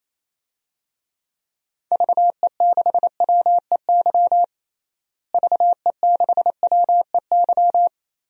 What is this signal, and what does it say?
Morse code, heard as a keyed tone, about 27 wpm, 700 Hz. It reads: VE6WEY VE6WEY
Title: DXCC-100 Call Signs - 6 Characters - Repeated 2x - 28wpm